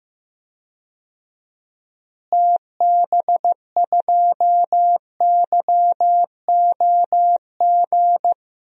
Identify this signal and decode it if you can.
Morse code, heard as a keyed tone, about 15 wpm, 700 Hz. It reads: TB2YOG
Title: DXCC-100 Call Signs - 6 Characters - Repeated 2x - 15wpm